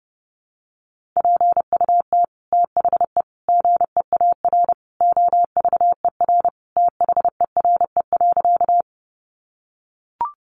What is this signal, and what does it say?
Morse code, heard as a keyed tone, about 30 wpm, 700 Hz. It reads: PUT THE GEAR OVER THERE.
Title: Sentences from Top 1400 Words 30wpm